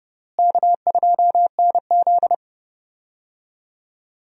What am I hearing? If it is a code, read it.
Morse code, heard as a keyed tone, about 32 wpm, 700 Hz. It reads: K2NZ
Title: USA Call Signs - 4 Characters - Repeated 3x - 30wpm